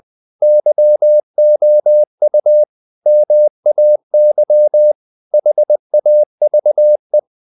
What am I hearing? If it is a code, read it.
Morse code, heard as a keyed tone, about 20 wpm, 600 Hz. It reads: YOU MAY HAVE
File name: the old man and the sea part 2 20wpm